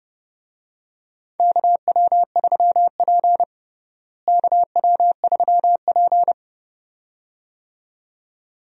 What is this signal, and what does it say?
Morse code, heard as a keyed tone, about 32 wpm, 700 Hz. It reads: KW3P KW3P
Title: USA Call Signs - 4 Characters - Repeated 2x - 30wpm